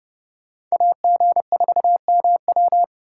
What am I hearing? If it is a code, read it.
Morse code, heard as a keyed tone, about 34 wpm, 700 Hz. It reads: AG4MW